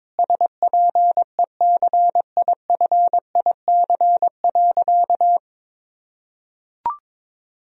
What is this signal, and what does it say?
Morse code, heard as a keyed tone, about 22 wpm, 700 Hz. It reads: SPECIFIC.
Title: Sentences from Top 1000 Words - Review Part 3 - 22wpm